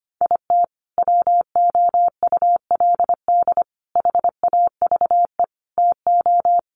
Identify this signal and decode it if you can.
Morse code, heard as a keyed tone, about 25 wpm, 700 Hz. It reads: IT WOULD HAVE TO
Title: Sets of 5 Words Encore 25wpm